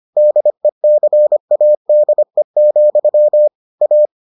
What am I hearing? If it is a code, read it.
Morse code, heard as a keyed tone, about 25 wpm, 600 Hz. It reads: DECADE, A